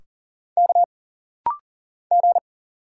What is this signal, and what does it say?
Morse code, heard as a keyed tone, about 40 wpm, 700 Hz. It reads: K G
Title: Single Letters - Rapid-Fire 40wpm